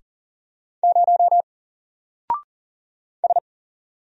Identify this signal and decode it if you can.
Morse code, heard as a keyed tone, about 40 wpm, 700 Hz. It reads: 0 S
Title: Single Character (Letter:Number:Punctuation:Prosign) 40wpm